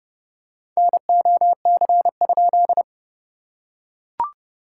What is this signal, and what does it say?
Morse code, heard as a keyed tone, about 30 wpm, 660 Hz. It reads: NOC?
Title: SOTA QSO Practice 30wpm